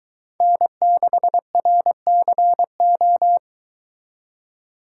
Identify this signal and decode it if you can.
Morse code, heard as a keyed tone, about 23 wpm, 700 Hz. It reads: N6RCO